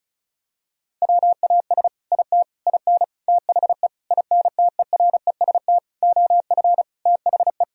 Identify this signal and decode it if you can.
Morse code, heard as a keyed tone, about 36 wpm, 700 Hz. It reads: WAS IT IN THE INTEREST OF THE
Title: Sentences from Top 1000 Words - Review Part 4 - 35wpm